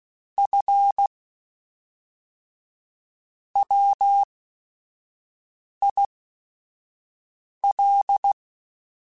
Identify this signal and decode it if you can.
Morse code, heard as a keyed tone, about 16 wpm, spaced as for 5 wpm, 800 Hz. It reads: F WIL
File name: Technical Subjects / Technical Subjects-mp3s / XL @5